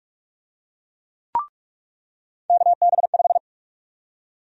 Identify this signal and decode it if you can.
Morse code, heard as a keyed tone, about 43 wpm, 700 Hz. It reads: KB5
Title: USA Call Sign Prefixes - 45wpm